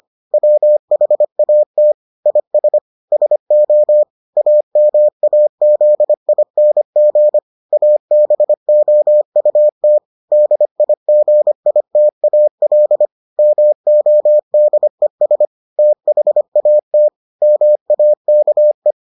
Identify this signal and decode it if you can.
Morse code, heard as a keyed tone, about 25 wpm, 600 Hz. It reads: WHAT IS SO AMAZING ABOUT DIGITAL MODES THAT MAKE